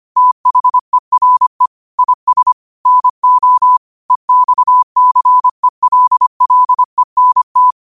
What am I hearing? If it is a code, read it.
Morse code, heard as a keyed tone, about 25 wpm, 1000 Hz. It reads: THERE IS NO EXCELLENT